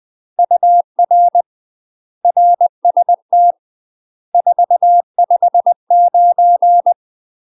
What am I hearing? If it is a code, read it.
Morse code, heard as a keyed tone, about 20 wpm, 700 Hz. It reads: UR RST 459
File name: UK MASTER QSOs TRAINING / 20WPM / UK MASTER QSOs 0053